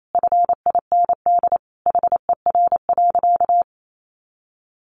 Morse code, heard as a keyed tone, about 28 wpm, 700 Hz. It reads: FIND HER.
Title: Sentences from Top 1300 Words 28wpm